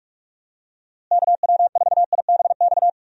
Morse code, heard as a keyed tone, about 41 wpm, 700 Hz. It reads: KW4IBX
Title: USA Call Signs - 6 Characters - Repeated 3x - 45wpm